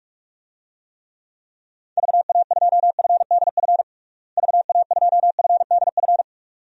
Morse code, heard as a keyed tone, about 46 wpm, 700 Hz. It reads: VA2FDF VA2FDF
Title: DXCC-100 Call Signs - 6 Characters - Repeated 2x - 45wpm